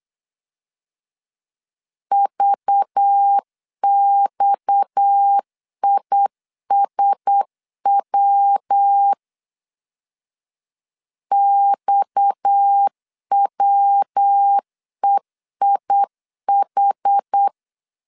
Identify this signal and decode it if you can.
Morse code, heard as a keyed tone, about 8 wpm, 800 Hz. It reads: VXISW XWEIH
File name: LEZIONI / 6 PKX0 / 6 PKX 8wpm 3041